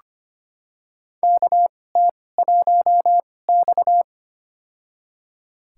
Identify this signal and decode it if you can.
Morse code, heard as a keyed tone, about 25 wpm, spaced as for 18 wpm, 700 Hz. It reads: KT1X
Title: USA Call Signs - 4 Characters - 2x Character Spacing - 25wpm